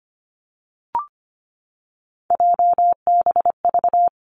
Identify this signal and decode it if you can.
Morse code, heard as a keyed tone, about 25 wpm, 700 Hz. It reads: JBV
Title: Three Challenging Letter-Number Combinations 25wpm